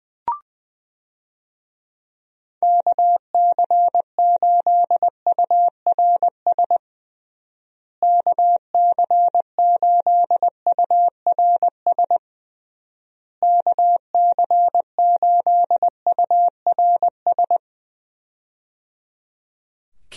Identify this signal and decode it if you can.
Morse code, heard as a keyed tone, about 20 wpm, 700 Hz. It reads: KC8URS KC8URS KC8URS
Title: USA Call Signs - 6 Characters - Repeated 3x - 20wpm